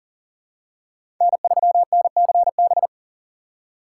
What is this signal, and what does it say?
Morse code, heard as a keyed tone, about 40 wpm, 700 Hz. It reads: N3NCB
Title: USA Call Signs - 4x Character Spacing - 40wpm